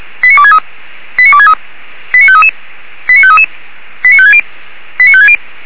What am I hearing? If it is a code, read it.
Five-tone selective call (ZVEI), heard as quick runs of five tones, tones of about 70 ms. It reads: 89263 89263 89533 89533 89560 89560